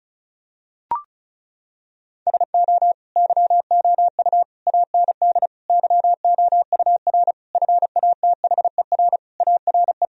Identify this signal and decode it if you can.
Morse code, heard as a keyed tone, about 36 wpm, 700 Hz. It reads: SO YOU AND YOUR FATHER ARE
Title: Sentences from Top 1000 Words - Review Part 4 - 35wpm